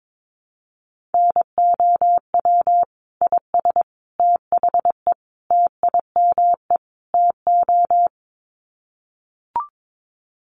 Morse code, heard as a keyed tone, about 22 wpm, 700 Hz. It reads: NOW IS THE TIME TO
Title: Sets of 5 Words Encore2 22wpm